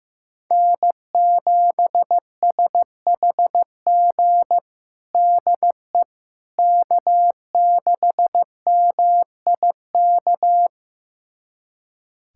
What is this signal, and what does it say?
Morse code, heard as a keyed tone, about 15 wpm, 700 Hz. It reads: N7SHG DE K6MIK
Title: USA Callsign DE Callsign - 5 Characters - 15wpm